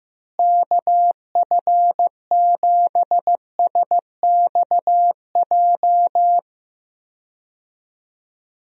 Morse code, heard as a keyed tone, about 15 wpm, 700 Hz. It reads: KF7SXJ
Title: USA Call Signs - 6 Characters - Repeated 2x - 15wpm